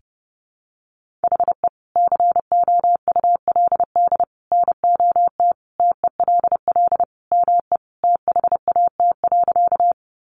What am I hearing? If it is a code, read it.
Morse code, heard as a keyed tone, about 30 wpm, 700 Hz. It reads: HE COULD NOT TELL ME THAT.
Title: Sentences from Top 600 Words 30wpm